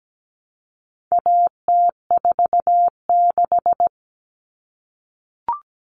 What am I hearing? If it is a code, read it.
Morse code, heard as a keyed tone, about 17 wpm, 700 Hz. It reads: AT46
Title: All Cut Numbers - 4-Digit 17wpm